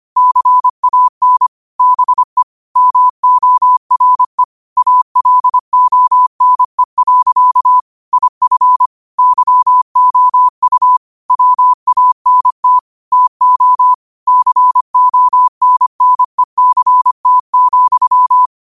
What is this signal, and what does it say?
Morse code, heard as a keyed tone, about 25 wpm, 1000 Hz. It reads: CAN BE MORE ALONE. IF YOU WANT TO CONNECT,